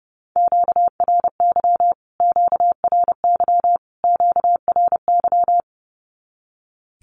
Morse code, heard as a keyed tone, about 30 wpm, 700 Hz. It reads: QRY QRY QRY
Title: Q-Codes Repeated 3x 30wpm